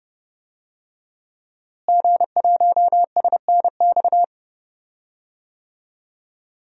Morse code, heard as a keyed tone, about 31 wpm, 700 Hz. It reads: G1SNX
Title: DXCC-100 Call Signs - 5 Characters - Repeated 3x - 30wpm